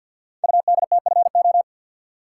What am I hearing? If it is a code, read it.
Morse code, heard as a keyed tone, about 48 wpm, 700 Hz. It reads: UDTFO
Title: Code Groups - 5 Letters - 50wpm